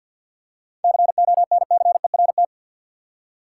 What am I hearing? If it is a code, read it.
Morse code, heard as a keyed tone, about 53 wpm, 700 Hz. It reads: CONCERT
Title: Seven Letter Words - 50wpm